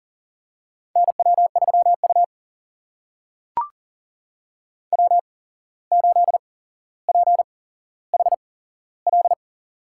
Morse code, heard as a keyed tone, about 40 wpm, spaced as for 16 wpm, 700 Hz. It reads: NW3U W8PHL
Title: USA Call Signs - 8x Character Spacing - 40wpm